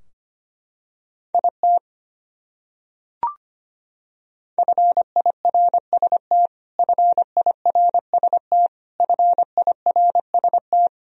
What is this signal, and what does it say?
Morse code, heard as a keyed tone, about 25 wpm, 700 Hz. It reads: IT FIRST FIRST FIRST
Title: Top 100 Words Repeated 25wpm